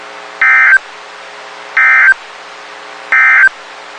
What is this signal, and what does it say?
Emergency Alert System (SAME) tones, heard as harsh buzzing bursts. It is heard from 0.3 to 3.6 s.
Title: WXK91 NOAA Weather Radio